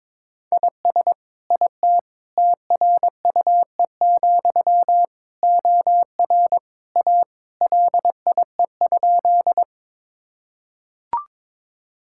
Morse code, heard as a keyed tone, about 22 wpm, 700 Hz. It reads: IS IT TRUE, OR A LIE?
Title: Sentences from Top 1600 Words 22wpm